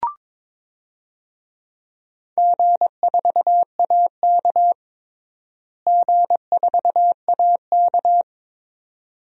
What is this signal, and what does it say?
Morse code, heard as a keyed tone, about 22 wpm, 700 Hz. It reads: G4AK G4AK
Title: DXCC-100 Call Signs - 4 Characters - Repeated 2x - 22wpm